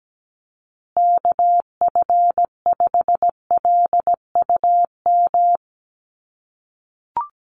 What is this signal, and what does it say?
Morse code, heard as a keyed tone, about 17 wpm, 700 Hz. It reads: KF5LUM
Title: USA Call Signs Repeated 2x 17wpm